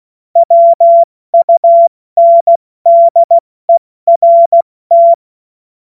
Morse code, heard as a keyed tone, about 16 wpm, 680 Hz. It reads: WUNDERT
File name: Morsekurs / Witze / Witz 12 März 2024 70 F80